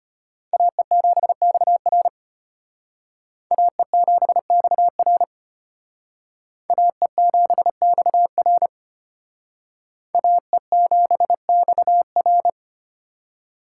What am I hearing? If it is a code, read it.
Morse code, heard as a keyed tone, about 31 wpm, 700 Hz. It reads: AE7XR AE7XR AE7XR AE7XR